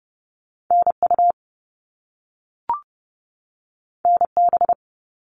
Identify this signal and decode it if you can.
Morse code, heard as a keyed tone, about 30 wpm, 700 Hz. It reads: NU NB
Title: Canadian Provinces and Territories 30wpm